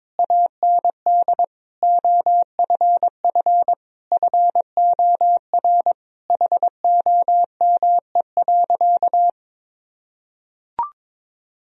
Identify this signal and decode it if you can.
Morse code, heard as a keyed tone, about 22 wpm, 700 Hz. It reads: AND OFF FOR HOME.
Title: Sentences from Top 1000 Words - Review Part 4 - 22wpm